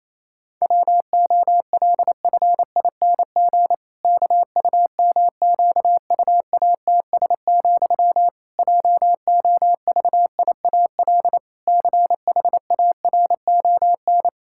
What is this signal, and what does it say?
Morse code, heard as a keyed tone, about 27 wpm, 700 Hz. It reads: WOLFING KUMQUATS, JOVIAL CHARON
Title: Pangram Challenge v2 - 28wpm